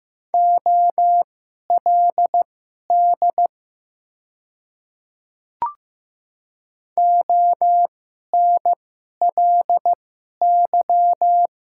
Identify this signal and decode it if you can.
Morse code, heard as a keyed tone, about 15 wpm, spaced as for 11 wpm, 690 Hz. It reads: OLD ONLY
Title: 2x Character Spacing - Top 200 Words - 15wpm